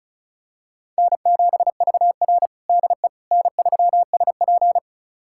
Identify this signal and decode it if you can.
Morse code, heard as a keyed tone, about 34 wpm, 700 Hz. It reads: N7VR DE N3SP
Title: USA Callsign DE Callsign - 4 Characters - 35wpm